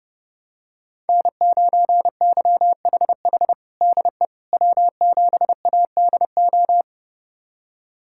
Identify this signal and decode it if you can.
Morse code, heard as a keyed tone, about 32 wpm, 700 Hz. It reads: N9YHH DE W7ADO
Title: USA Callsign DE Callsign - 5 Characters - 30wpm